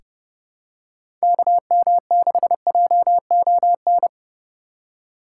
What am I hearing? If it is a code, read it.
Morse code, heard as a keyed tone, about 32 wpm, 700 Hz. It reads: KM6JON
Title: USA Call Signs - 6 Characters - 2x Character Spacing - 30wpm